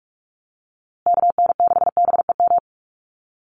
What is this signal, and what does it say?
Morse code, heard as a keyed tone, about 44 wpm, 690 Hz. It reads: KN6BEM
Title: USA Call Signs Repeated 3x 45wpm